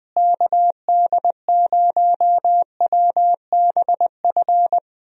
Morse code, heard as a keyed tone, about 20 wpm, 700 Hz. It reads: KD0WBF
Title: USA Call Signs - 6 Characters - Repeated 3x - 20wpm